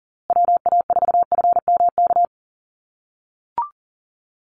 Morse code, heard as a keyed tone, about 39 wpm, 700 Hz. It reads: WA4FMK
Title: USA Call Signs Repeated 3x 40wpm